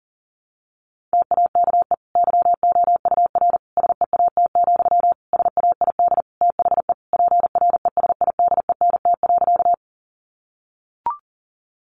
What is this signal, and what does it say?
Morse code, heard as a keyed tone, about 40 wpm, 700 Hz. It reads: TAKE YOUR SEAT, SAID THE PRESIDENT.
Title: Sentences from Top 1500 Words 40wpm